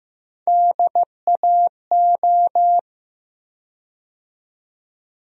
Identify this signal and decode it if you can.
Morse code, heard as a keyed tone, about 15 wpm, 700 Hz. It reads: DAO
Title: USA Call Sign Postfixes - 3 Letters 15wpm